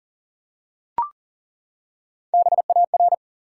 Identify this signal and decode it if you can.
Morse code, heard as a keyed tone, about 38 wpm, 700 Hz. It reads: BAR
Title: Three Letter Words - 40wpm